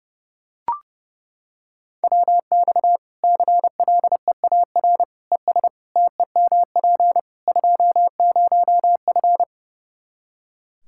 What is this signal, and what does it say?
Morse code, heard as a keyed tone, about 30 wpm, 700 Hz. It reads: WX CLEAR ES TEMP 20F
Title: Practice QSO - Exchange 2 - Info - 30wpm